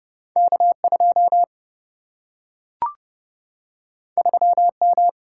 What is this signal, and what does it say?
Morse code, heard as a keyed tone, about 31 wpm, 700 Hz. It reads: K 2 3 M
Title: Single-Letter Single-Number 30wpm